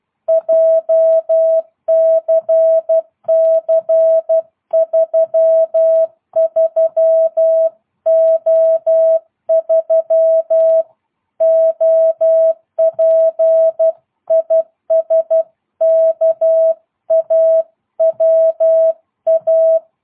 Morse code, heard as a keyed tone, about 12 wpm, 660 Hz. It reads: JCC33O3 OP IS KAWA
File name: jh8cbh / 2022cw